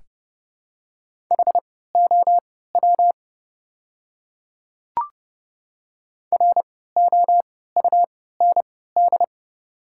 Morse code, heard as a keyed tone, about 30 wpm, spaced as for 17 wpm, 700 Hz. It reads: HOW ROUND
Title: 3x Character Spacing - Top 200 Words - 30wpm